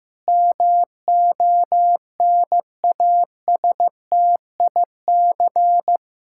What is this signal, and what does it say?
Morse code, heard as a keyed tone, about 15 wpm, 700 Hz. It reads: MONASTIC
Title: Pangram Challenge v2 - 15wpm